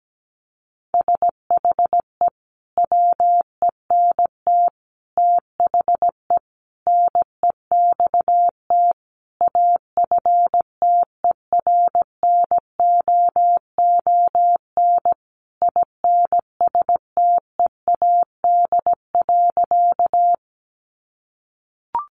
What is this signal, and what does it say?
Morse code, heard as a keyed tone, about 17 wpm, 700 Hz. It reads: SHE WENT THE NEXT AFTERNOON INSTEAD.
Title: Sentences from Top 1500 Words 17wpm